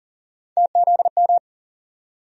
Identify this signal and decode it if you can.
Morse code, heard as a keyed tone, about 40 wpm, 700 Hz. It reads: TZM
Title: USA Call Sign Postfixes - 3 Letters 40wpm